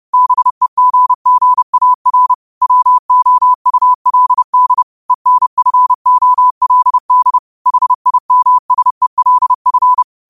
Morse code, heard as a keyed tone, about 30 wpm, 1000 Hz. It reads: BEGGAR WOULD ENFOLD HIMSELF
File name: qotdfiles / 30wpm